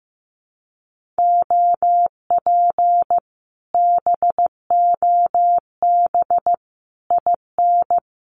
Morse code, heard as a keyed tone, about 15 wpm, 700 Hz. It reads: OP BOB IN
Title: Common QSO Phrases Encore 15wpm